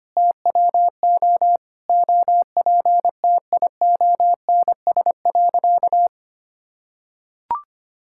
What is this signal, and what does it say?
Morse code, heard as a keyed tone, about 25 wpm, 700 Hz. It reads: TWO OPTIONS.
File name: practice / 25wpm / Sentences with 3-Word Phrases - 25wpm